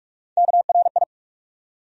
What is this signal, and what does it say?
Morse code, heard as a keyed tone, about 46 wpm, 700 Hz. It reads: KRI
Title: USA Call Sign Postfixes - 3 Letters 45wpm